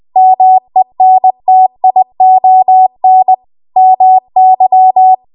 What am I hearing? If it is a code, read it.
Morse code, heard as a keyed tone, about 20 wpm, 750 Hz. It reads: MENTION MY
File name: Archive / 20 WPM / 20WPM